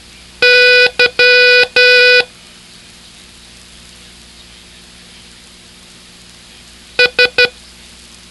Morse code, heard as a keyed tone, about 11 wpm, 1480 Hz. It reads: Y S